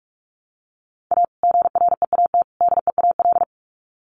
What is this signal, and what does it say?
Morse code, heard as a keyed tone, about 44 wpm, 700 Hz. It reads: A GREAT DEAL
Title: Sets of 3 Words 45wpm